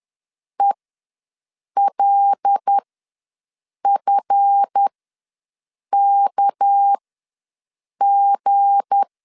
Morse code, heard as a keyed tone, about 10 wpm, spaced as for 6 wpm, 800 Hz. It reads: ELFKG
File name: LEZIONI / 7 GF9Z / 7 GF9Z ws 10wpm 3919